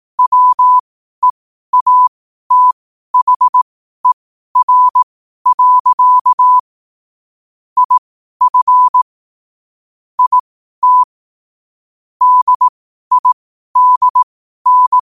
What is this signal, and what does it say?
Morse code, heard as a keyed tone, about 18 wpm, spaced as for 12 wpm, 1000 Hz. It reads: WEATHER. IF IT DIDN
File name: qotdfiles / 12wpm